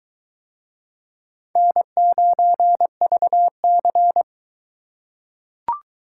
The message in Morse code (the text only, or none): N9VC